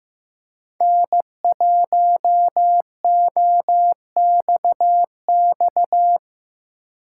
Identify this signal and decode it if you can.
Morse code, heard as a keyed tone, about 15 wpm, 700 Hz. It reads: N1OXX
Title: Speed Racing - USA 1x3 Call Signs 15wpm